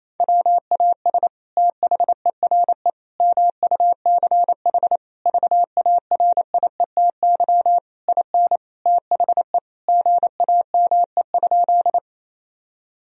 Morse code, heard as a keyed tone, about 28 wpm, 700 Hz. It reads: WAS THERE MUCH VARIETY IN THE GAME?
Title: Sentences from Top 1000 Words - Review Part 4 - 28wpm